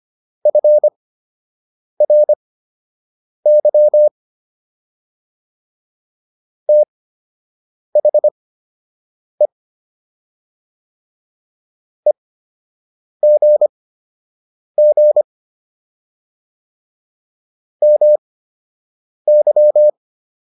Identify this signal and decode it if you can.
Morse code, heard as a keyed tone, about 25 wpm, spaced as for 7 wpm, 600 Hz. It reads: FRY THE EGG MY